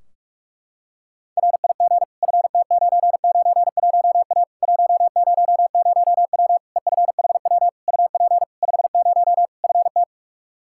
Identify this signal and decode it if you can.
Morse code, heard as a keyed tone, about 47 wpm, 700 Hz. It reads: RIG FT991A 100W EFHW UP 50 FT
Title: Practice QSO - Exchange 2 - Info - 45wpm